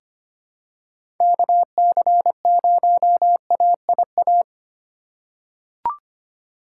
Morse code, heard as a keyed tone, about 25 wpm, 700 Hz. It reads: KC0AIA
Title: USA Call Signs - 6 Characters - 3x Character Spacing - 25wpm